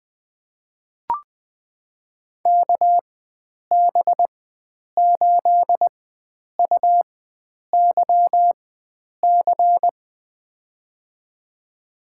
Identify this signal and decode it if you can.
Morse code, heard as a keyed tone, about 20 wpm, spaced as for 9 wpm, 700 Hz. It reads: KB8UYC
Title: USA Call Signs - 6 Characters - 4x Character Spacing - 20wpm